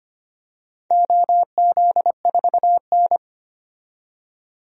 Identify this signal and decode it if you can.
Morse code, heard as a keyed tone, about 25 wpm, 690 Hz. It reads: OZ4N
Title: DXCC-100 Call Signs - 4 Characters - 25wpm